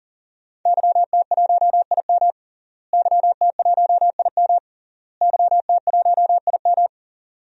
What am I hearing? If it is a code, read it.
Morse code, heard as a keyed tone, about 37 wpm, 700 Hz. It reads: YT1IM YT1IM YT1IM
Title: DXCC-100 Call Signs - 5 Characters - Repeated 3x - 40wpm